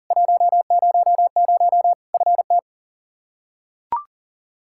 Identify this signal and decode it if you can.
Morse code, heard as a keyed tone, about 40 wpm, 700 Hz. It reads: 100 FT
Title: Practice QSO - Exchange 2 - Info - 40wpm